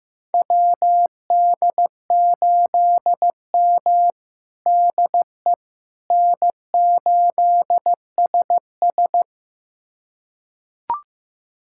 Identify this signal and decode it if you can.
Morse code, heard as a keyed tone, about 15 wpm, 700 Hz. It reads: WD8M DE N8SS
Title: USA Callsign DE Callsign - 4 Characters - 15wpm